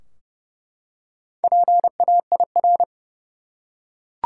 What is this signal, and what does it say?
Morse code, heard as a keyed tone, about 30 wpm, 700 Hz. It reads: PAIR
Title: Four Letter Words - 30wpm